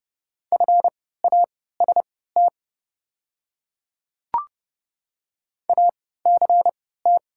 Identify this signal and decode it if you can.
Morse code, heard as a keyed tone, about 31 wpm, spaced as for 17 wpm, 700 Hz. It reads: FAST ACT
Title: 3x Character Spacing - Top 500 Words - 30wpm